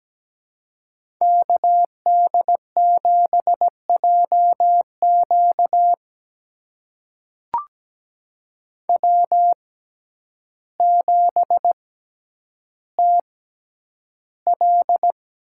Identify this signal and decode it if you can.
Morse code, heard as a keyed tone, about 17 wpm, spaced as for 9 wpm, 700 Hz. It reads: KD7JQ W7TL
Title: USA Call Signs - 5 Characters - 6x Character Spacing - 17wpm